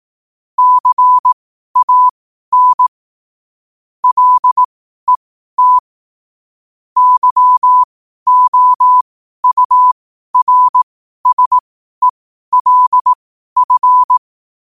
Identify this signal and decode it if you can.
Morse code, heard as a keyed tone, about 18 wpm, spaced as for 12 wpm, 1000 Hz. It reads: CAN LET YOURSELF
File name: qotdfiles / 12wpm